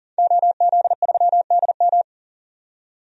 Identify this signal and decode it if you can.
Morse code, heard as a keyed tone, about 40 wpm, 700 Hz. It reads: OZ3DM
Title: DXCC-100 Call Signs - 5 Characters - 40wpm